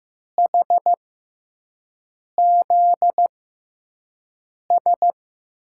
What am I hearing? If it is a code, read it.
Morse code, heard as a keyed tone, about 15 wpm, spaced as for 5 wpm, 700 Hz. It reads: HZS
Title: USA Call Signs - 5 Characters - 6x Character Spacing - 15wpm